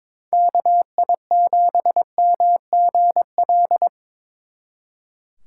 Morse code, heard as a keyed tone, about 25 wpm, 700 Hz. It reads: KI7MGL
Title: Speed Racing - USA 2x3 Call Signs 22wpm